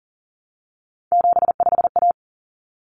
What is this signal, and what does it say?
Morse code, heard as a keyed tone, about 40 wpm, 700 Hz. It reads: 75A
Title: Common Cut Numbers - 3-Digit 40wpm